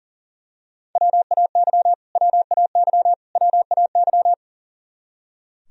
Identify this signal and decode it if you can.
Morse code, heard as a keyed tone, about 40 wpm, 700 Hz. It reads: WAY WAY WAY
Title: Top 100 Words Repeated 40wpm